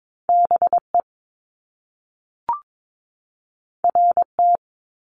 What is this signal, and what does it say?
Morse code, heard as a keyed tone, about 22 wpm, 700 Hz. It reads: BE RT
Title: Top Two Letter Combinations in Words 22wpm